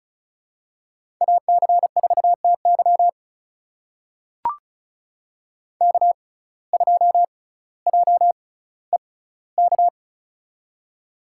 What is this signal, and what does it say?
Morse code, heard as a keyed tone, about 36 wpm, spaced as for 18 wpm, 700 Hz. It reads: AC4TY K2JEK
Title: USA Call Signs - 5 Characters - 6x Character Spacing - 35wpm